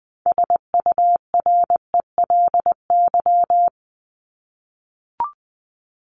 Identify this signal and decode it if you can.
Morse code, heard as a keyed tone, about 20 wpm, 700 Hz. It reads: SURELY
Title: LY Suffix - Top 100 Words with Base Words 20wpm